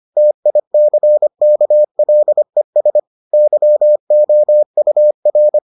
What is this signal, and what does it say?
Morse code, heard as a keyed tone, about 25 wpm, 600 Hz. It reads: TICKLES YOUR